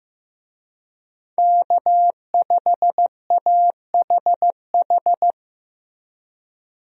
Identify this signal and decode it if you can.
Morse code, heard as a keyed tone, about 15 wpm, 700 Hz. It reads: K5AHH
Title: USA Call Signs - 5 Characters - Repeated 3x - 15wpm